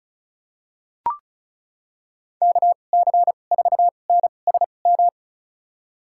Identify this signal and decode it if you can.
Morse code, heard as a keyed tone, about 34 wpm, spaced as for 25 wpm, 700 Hz. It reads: KC4NSM
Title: USA Call Signs - 6 Characters - 2x Character Spacing - 35wpm